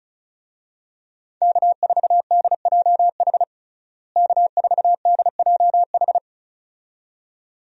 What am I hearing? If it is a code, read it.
Morse code, heard as a keyed tone, about 34 wpm, 700 Hz. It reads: K4DJH K4DJH
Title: USA Call Signs - 5 Characters - Repeated 2x - 35wpm